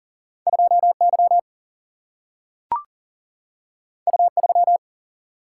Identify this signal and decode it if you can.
Morse code, heard as a keyed tone, about 38 wpm, 700 Hz. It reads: 2 Y U 3
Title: Single-Letter Single-Number 40wpm